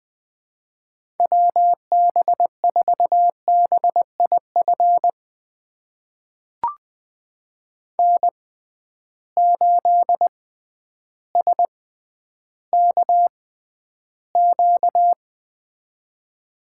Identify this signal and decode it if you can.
Morse code, heard as a keyed tone, about 20 wpm, spaced as for 11 wpm, 700 Hz. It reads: WB4BIF N8SKQ
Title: USA Call Signs - 6x Character Spacing - 20wpm